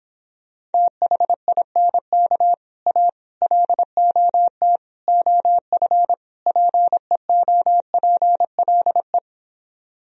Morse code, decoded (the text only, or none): THINK A LOT OF PEOPLE